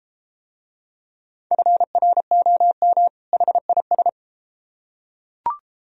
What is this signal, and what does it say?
Morse code, heard as a keyed tone, about 32 wpm, 700 Hz. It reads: FROM HIS